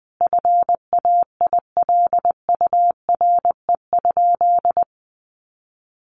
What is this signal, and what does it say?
Morse code, heard as a keyed tone, about 20 wpm, 700 Hz. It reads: FAILURE?
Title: Sentences from Top 1200 Words 20wpm